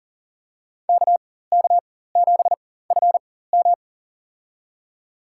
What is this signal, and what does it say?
Morse code, heard as a keyed tone, about 40 wpm, spaced as for 19 wpm, 700 Hz. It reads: KK7FM
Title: USA Call Signs - 4x Character Spacing - 40wpm